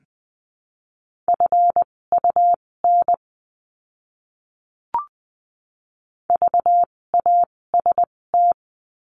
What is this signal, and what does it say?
Morse code, heard as a keyed tone, about 20 wpm, spaced as for 16 wpm, 700 Hz. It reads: FUN VAST
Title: 256 Lesson 42a - Words 12-20wpm